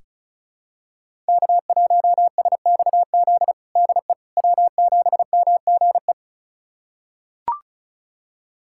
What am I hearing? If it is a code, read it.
Morse code, heard as a keyed tone, about 36 wpm, 700 Hz. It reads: K1SXZ DE W7MGE
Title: USA Callsign DE Callsign - 5 Characters - 35wpm